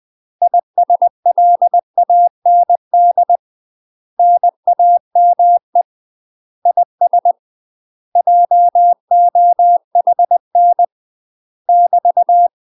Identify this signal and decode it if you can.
Morse code, heard as a keyed tone, about 20 wpm, 700 Hz. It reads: ISLAND NAME IS JOHN <BT>
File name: UK MASTER QSOs TRAINING/20WPM